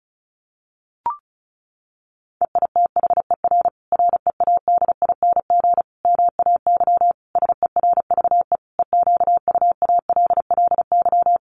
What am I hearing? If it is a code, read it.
Morse code, heard as a keyed tone, about 34 wpm, 700 Hz. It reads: EITHER READING MAY SERVE EQUALLY
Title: Sentences from Top 1400 Words 35wpm